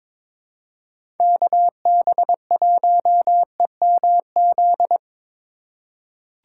Morse code, heard as a keyed tone, about 23 wpm, 700 Hz. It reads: KB1EMZ
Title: USA Call Signs - 6 Characters - 2x Character Spacing - 22wpm